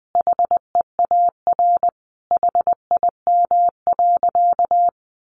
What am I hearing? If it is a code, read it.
Morse code, heard as a keyed tone, about 20 wpm, 700 Hz. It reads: HEAR HIM.
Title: Sentences from Top 1000 Words 20wpm